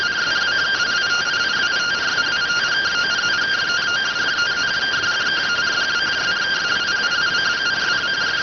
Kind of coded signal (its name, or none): two-tone teleprinter signal